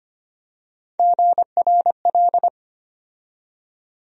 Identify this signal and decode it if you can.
Morse code, heard as a keyed tone, about 25 wpm, 700 Hz. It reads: GRL
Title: USA Call Sign Postfixes - 3 Letters 25wpm